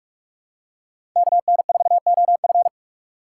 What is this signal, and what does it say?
Morse code, heard as a keyed tone, about 46 wpm, 700 Hz. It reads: KN4OF
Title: USA Call Signs - 3x Character Spacing - 45wpm